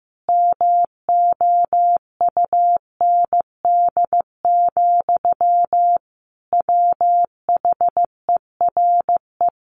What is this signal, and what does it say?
Morse code, heard as a keyed tone, about 14 wpm, 700 Hz. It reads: MOUND, WHERE
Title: A Sound Of Thunder 15wpm